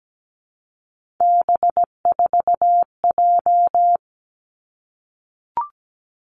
Morse code, heard as a keyed tone, about 17 wpm, 700 Hz. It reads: B4J
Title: Three Challenging Letter-Number Combinations 17wpm